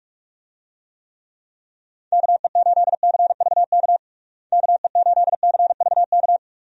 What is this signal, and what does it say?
Morse code, heard as a keyed tone, about 46 wpm, 700 Hz. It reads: KE8CVK KE8CVK
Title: USA Call Signs - 6 Characters - Repeated 2x - 45wpm